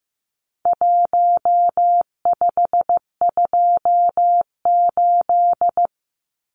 Morse code, heard as a keyed tone, about 15 wpm, 700 Hz. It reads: 1528
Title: Numbers - Contest Style - 4-Digit 15wpm